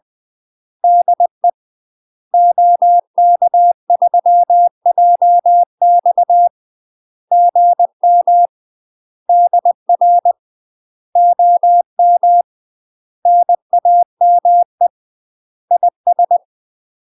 Morse code, heard as a keyed tone, about 20 wpm, 700 Hz. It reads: DE OK3JX GM DR OM NAME IS
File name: UK MASTER QSOs TRAINING / 20WPM / UK MASTER QSOs 0038